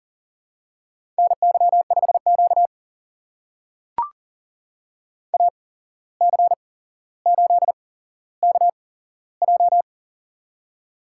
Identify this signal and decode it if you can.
Morse code, heard as a keyed tone, about 38 wpm, spaced as for 16 wpm, 700 Hz. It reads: NY5Q AC8KJ
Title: USA Call Signs - 8x Character Spacing - 40wpm